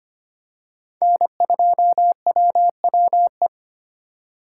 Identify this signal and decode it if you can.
Morse code, heard as a keyed tone, about 25 wpm, 700 Hz. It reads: N2WWE
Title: USA Call Signs - 2x Character Spacing - 25wpm